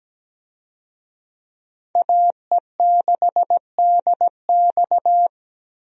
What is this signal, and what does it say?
Morse code, heard as a keyed tone, about 17 wpm, 700 Hz. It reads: AE6DX
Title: USA Call Signs - 5 Characters - Repeated 3x - 17wpm